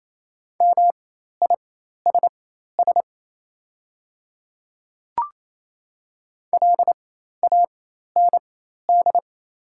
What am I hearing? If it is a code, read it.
Morse code, heard as a keyed tone, about 28 wpm, spaced as for 13 wpm, 700 Hz. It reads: MISS LAND
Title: 4x Character Spacing - Top 500 Words - 28wpm